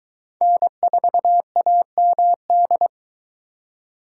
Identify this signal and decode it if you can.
Morse code, heard as a keyed tone, about 22 wpm, 700 Hz. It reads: N4AMD